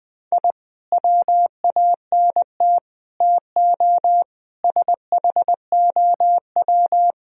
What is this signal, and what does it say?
Morse code, heard as a keyed tone, about 21 wpm, 700 Hz. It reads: I WANT TO SHOW
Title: Speed-Racing - Sets of 5 Words 20wpm